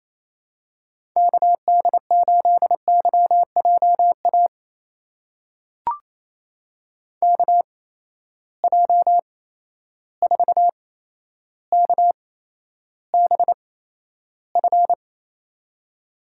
Morse code, heard as a keyed tone, about 28 wpm, spaced as for 12 wpm, 700 Hz. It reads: KD8YJA KJ4KBF
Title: USA Call Signs - 8x Character Spacing - 28wpm